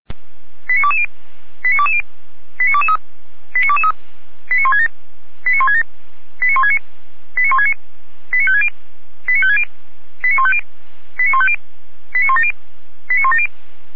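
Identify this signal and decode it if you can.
Five-tone selective call (ZVEI), heard as quick runs of five tones, tones of about 70 ms. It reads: 89220 89220 89223 89223 89167 89167 89169 89169 89560 89560 89150 89150 89170 89170
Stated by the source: ZVEI five-tone selective call received on an SDR.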